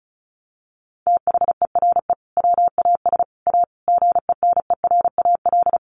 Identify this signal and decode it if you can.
Morse code, heard as a keyed tone, about 34 wpm, 700 Hz. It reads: THERE WAS A GENERAL
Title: Sentences from Top 1300 Words 35wpm